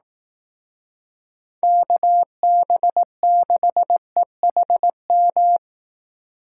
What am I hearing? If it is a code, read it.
Morse code, heard as a keyed tone, about 18 wpm, 700 Hz. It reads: KB6EHM
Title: Speed Racing - USA 2x3 Call Signs 15wpm